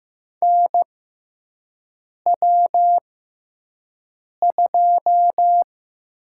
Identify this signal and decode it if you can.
Morse code, heard as a keyed tone, about 15 wpm, spaced as for 5 wpm, 700 Hz. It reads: NW2
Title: USA Call Signs - 6x Character Spacing - 15wpm